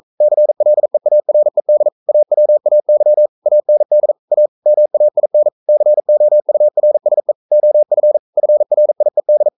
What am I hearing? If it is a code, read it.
Morse code, heard as a keyed tone, about 42 wpm, 600 Hz. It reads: CLEARED AWAY AND A MAIN COURSE OF FRIED
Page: Romanen Never af Ken Follet på engelsk.